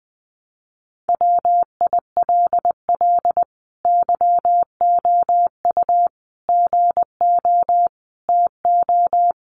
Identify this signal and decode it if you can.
Morse code, heard as a keyed tone, about 20 wpm, 700 Hz. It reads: WILL YOU GO TO
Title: Sentences from Top 1300 Words 20wpm